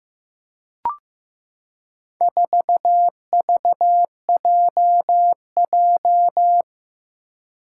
Morse code, heard as a keyed tone, about 15 wpm, 700 Hz. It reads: 4VJJ
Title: Four Challenging Letter-Number Combinations 15wpm